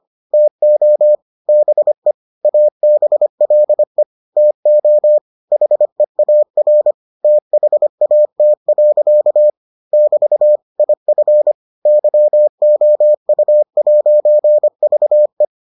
Morse code, heard as a keyed tone, about 25 wpm, 600 Hz. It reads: TO BE ABLE TO HEAR THAT. <BT> IF YOU'VE